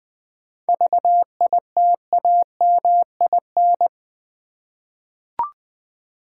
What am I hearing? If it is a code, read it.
Morse code, heard as a keyed tone, about 20 wpm, 700 Hz. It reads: VITAMIN
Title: 7 Letter Words - First 3 Letters Given - 20wpm